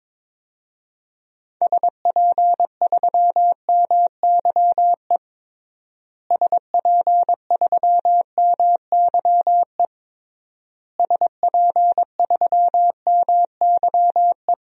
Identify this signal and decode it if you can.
Morse code, heard as a keyed tone, about 22 wpm, 700 Hz. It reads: SP3MYE SP3MYE SP3MYE
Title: DXCC-100 Call Signs - 6 Characters - Repeated 3x - 22wpm